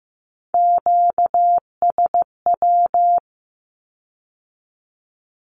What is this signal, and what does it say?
Morse code, heard as a keyed tone, about 15 wpm, 700 Hz. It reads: QSW
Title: Q-Codes with Spoken Meaning - 15wpm